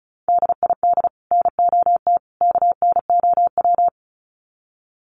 Morse code, heard as a keyed tone, about 34 wpm, 700 Hz. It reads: DID NOT KNOW
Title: Sets of 3 Words 35wpm